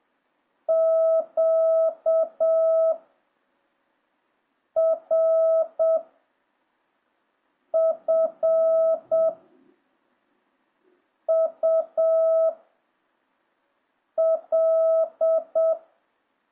Morse code, heard as a keyed tone, about 7 wpm, 650 Hz. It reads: Q R F U L